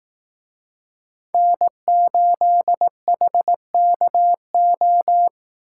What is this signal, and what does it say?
Morse code, heard as a keyed tone, about 18 wpm, 700 Hz. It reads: N8HKO
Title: Speed Racing - USA 1x3 Call Signs 15wpm